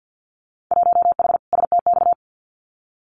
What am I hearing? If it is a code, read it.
Morse code, heard as a keyed tone, about 50 wpm, 700 Hz. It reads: 1H STX
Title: Winter Field Day Exchanges Encore 50wpm